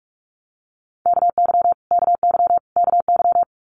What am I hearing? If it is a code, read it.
Morse code, heard as a keyed tone, about 44 wpm, 700 Hz. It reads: KY KY KY
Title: US States Repeated 3x 45wpm